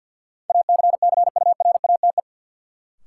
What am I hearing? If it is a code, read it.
Morse code, heard as a keyed tone, about 46 wpm, 700 Hz. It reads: ACCURATE
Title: Sentences from Top 1200 Words - New Word Drill - 50wpm